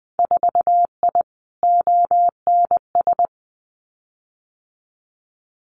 Morse code, heard as a keyed tone, about 20 wpm, 700 Hz. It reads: 4I ONS
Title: Winter Field Day Exchanges Encore 20wpm